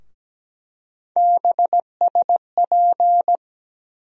Morse code, heard as a keyed tone, about 17 wpm, 700 Hz. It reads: BSP
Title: USA Call Sign Postfixes - 3 Letters 17wpm